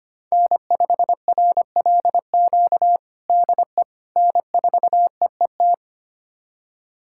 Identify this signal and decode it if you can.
Morse code, heard as a keyed tone, about 25 wpm, 700 Hz. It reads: N5RLQ DE N4EET
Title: USA Callsign DE Callsign - 5 Characters - 25wpm